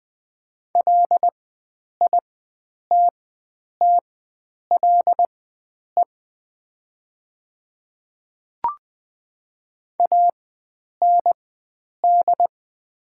Morse code, heard as a keyed tone, about 20 wpm, spaced as for 9 wpm, 700 Hz. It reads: LITTLE AND
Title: 4x Character Spacing - Top 300 Words - 20wpm